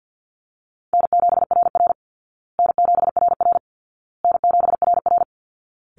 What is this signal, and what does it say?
Morse code, heard as a keyed tone, about 50 wpm, 700 Hz. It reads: N7RR N7RR N7RR
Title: Easy Call Signs Repeated 3x - 50wpm